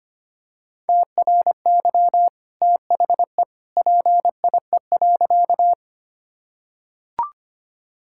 Morse code, heard as a keyed tone, about 25 wpm, 690 Hz. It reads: TRY THE PIE.
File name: Easy Sentences with Hints - 25wpm